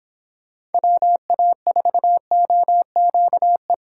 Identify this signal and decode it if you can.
Morse code, heard as a keyed tone, about 26 wpm, 700 Hz. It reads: WA4OQE